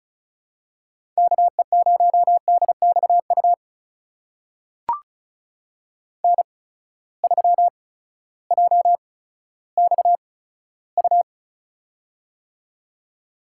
Morse code, heard as a keyed tone, about 36 wpm, spaced as for 16 wpm, 700 Hz. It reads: KE0DXU N3JXU
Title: USA Call Signs - 8x Character Spacing - 35wpm